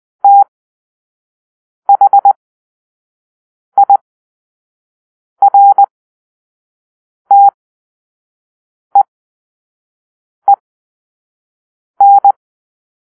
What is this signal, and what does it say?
Morse code, heard as a keyed tone, about 20 wpm, spaced as for 5 wpm, 800 Hz. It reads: THIRTEEN
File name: arrl / Grp-2 Drill-3